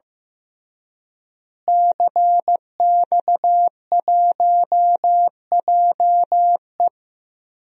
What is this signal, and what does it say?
Morse code, heard as a keyed tone, about 15 wpm, 700 Hz. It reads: CX1JE
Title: DXCC-100 Call Signs - 5 Characters - Repeated 3x - 15wpm